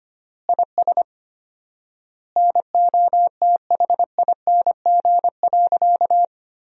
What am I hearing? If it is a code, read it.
Morse code, heard as a keyed tone, about 25 wpm, 700 Hz. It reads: IS NOTHING.
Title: 4x Word-Spacing - Sentences from Top 500 Words - 25wpm